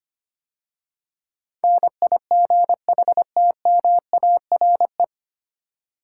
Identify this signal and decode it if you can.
Morse code, heard as a keyed tone, about 25 wpm, 700 Hz. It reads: NIGHTMARE
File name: practice / 25wpm / 9 Letter Words - First 3 Letters Given - 25wpm